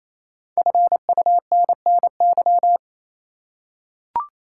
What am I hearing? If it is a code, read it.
Morse code, heard as a keyed tone, about 28 wpm, 700 Hz. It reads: FUNNY
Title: Five Letter Words - 28wpm